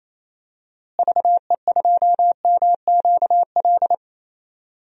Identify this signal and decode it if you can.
Morse code, heard as a keyed tone, about 28 wpm, 700 Hz. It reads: VE2MQL
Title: DXCC-100 Call Signs - 6 Characters - Repeated 3x - 28wpm